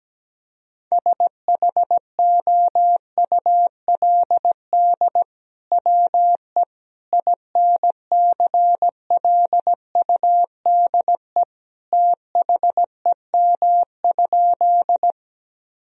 Morse code, heard as a keyed tone, about 17 wpm, 700 Hz. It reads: SHOULD WE INCLUDE THEM?
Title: Sentences from Top 1000 Words - Review Part 3 - 17wpm